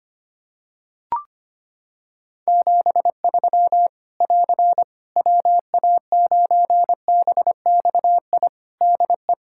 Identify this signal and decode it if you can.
Morse code, heard as a keyed tone, about 25 wpm, 700 Hz. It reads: 73 <AR> WA9BXI DE
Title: Practice QSO - Exchange 4 - Ending - 25wpm